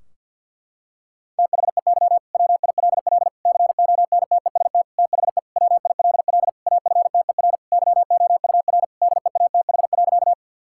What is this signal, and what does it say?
Morse code, heard as a keyed tone, about 50 wpm, 700 Hz. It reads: THEY WILL CONTEST THE WILL AFTER YOUR DEATH.
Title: Sentences from Top 1600 Words 50wpm